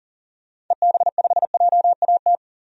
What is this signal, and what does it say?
Morse code, heard as a keyed tone, about 40 wpm, 700 Hz. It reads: EB5JAT
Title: DXCC-100 Call Signs - 6 Characters - 40wpm